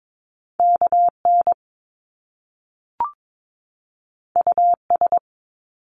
Morse code, heard as a keyed tone, about 22 wpm, 700 Hz. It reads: KN US
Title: Top Two Letter Combinations in Words 22wpm